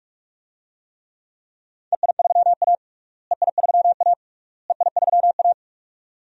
Morse code, heard as a keyed tone, about 46 wpm, 700 Hz. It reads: EI3A EI3A EI3A
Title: DXCC-100 Call Signs - 4 Characters - Repeated 3x - 45wpm